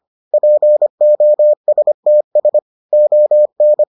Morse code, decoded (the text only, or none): POSTS ON